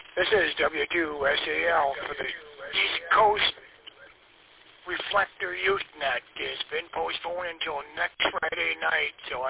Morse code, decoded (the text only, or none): none